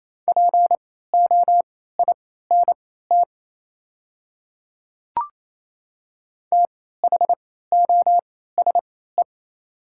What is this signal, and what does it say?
Morse code, heard as a keyed tone, about 28 wpm, spaced as for 16 wpm, 700 Hz. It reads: POINT THOSE
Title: 3x Character Spacing - Top 500 Words - 28wpm